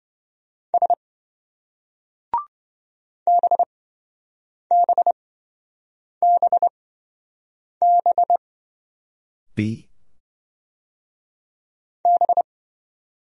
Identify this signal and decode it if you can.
Morse code, heard as a keyed tone, about 26 wpm, spaced as for 7 wpm, 700 Hz. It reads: S BBBB B